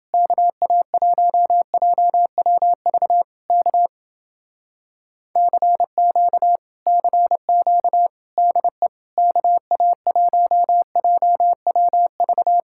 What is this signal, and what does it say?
Morse code, heard as a keyed tone, about 29 wpm, 700 Hz. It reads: KA1JWV K CQ CQ DE KA1JWV